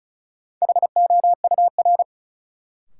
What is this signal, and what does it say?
Morse code, heard as a keyed tone, about 36 wpm, 700 Hz. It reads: HOUR
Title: Sets of 5 Words - New Word Drill - 35wpm